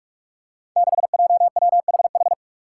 Morse code, heard as a keyed tone, about 46 wpm, 700 Hz. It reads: 6JWHH
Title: Five Challenging Letter-Number Combinations 45wpm